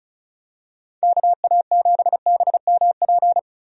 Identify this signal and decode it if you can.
Morse code, heard as a keyed tone, about 36 wpm, 700 Hz. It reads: KA7BMP
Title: USA Call Signs - 6 Characters - 35wpm